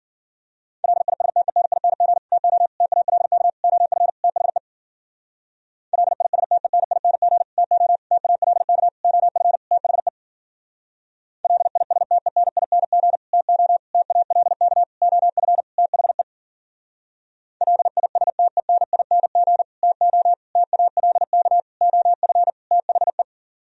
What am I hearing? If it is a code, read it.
Morse code, heard as a keyed tone, about 51 wpm, 700 Hz. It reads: LISTENING TO TALK OF THE LISTENING TO TALK OF THE LISTENING TO TALK OF THE LISTENING TO TALK OF THE